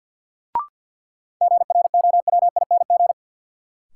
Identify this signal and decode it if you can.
Morse code, heard as a keyed tone, about 51 wpm, 700 Hz. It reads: GROWING
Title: Sets of 5 Words Encore - New Word Drill - 50wpm